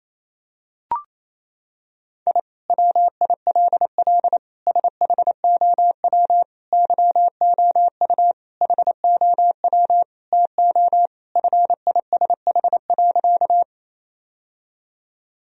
Morse code, heard as a keyed tone, about 28 wpm, 700 Hz. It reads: I WILL SHOW YOU HOW TO FISH.
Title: Sentences from Top 1000 Words - Review Part 1 - 28wpm